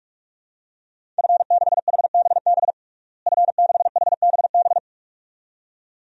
Morse code, heard as a keyed tone, about 46 wpm, 700 Hz. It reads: F6HBB F6HBB
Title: DXCC-100 Call Signs - 5 Characters - Repeated 2x - 45wpm